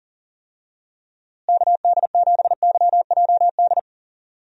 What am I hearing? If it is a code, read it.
Morse code, heard as a keyed tone, about 40 wpm, 700 Hz. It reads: KD7YJD
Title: USA Call Signs - 6 Characters - Repeated 2x - 40wpm